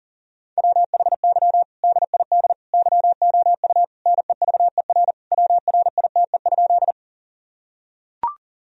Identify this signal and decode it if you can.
Morse code, heard as a keyed tone, about 37 wpm, 700 Hz. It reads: WHY DID YOU NEVER WRITE?
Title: Sentences from Top 1000 Words - Review Part 1 - 40wpm